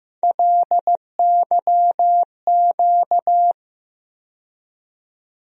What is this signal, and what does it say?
Morse code, heard as a keyed tone, about 16 wpm, 700 Hz. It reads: LYQ
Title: USA Call Sign Postfixes - 3 Letters 15wpm